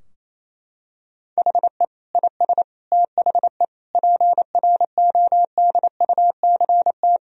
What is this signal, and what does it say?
Morse code, heard as a keyed tone, about 28 wpm, 700 Hz. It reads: HE IS THE PRODUCT
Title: Sentences from Top 1000 Words - Review Part 2 - 28wpm